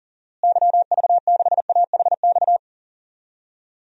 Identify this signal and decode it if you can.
Morse code, heard as a keyed tone, about 38 wpm, 700 Hz. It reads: YV6AHX
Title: DXCC-100 Call Signs - 6 Characters - Repeated 3x - 40wpm